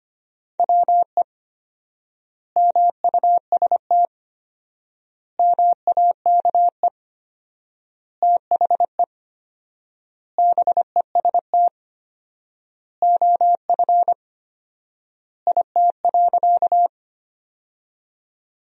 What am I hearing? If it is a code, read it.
Morse code, heard as a keyed tone, about 25 wpm, 700 Hz. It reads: WE MUST MAKE THE BEST OF IT.
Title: 4x Word-Spacing - Sentences from Top 400 Words - 25wpm